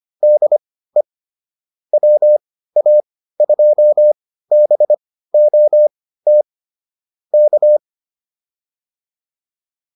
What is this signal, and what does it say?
Morse code, heard as a keyed tone, about 25 wpm, spaced as for 15 wpm, 600 Hz. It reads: DE WA2BOT K